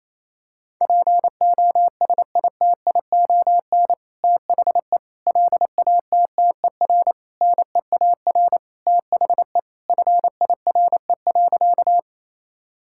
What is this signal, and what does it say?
Morse code, heard as a keyed tone, about 28 wpm, 700 Hz. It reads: POSITION THE LATTER NEAR THE FIRE.
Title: Sentences from Top 1600 Words 28wpm